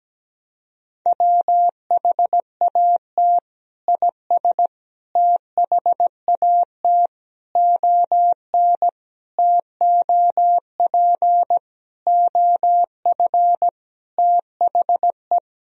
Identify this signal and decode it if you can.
Morse code, heard as a keyed tone, about 17 wpm, 700 Hz. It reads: WHAT IS THAT ON TOP OF THE
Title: Sentences from Top 1000 Words - Review Part 2 - 17wpm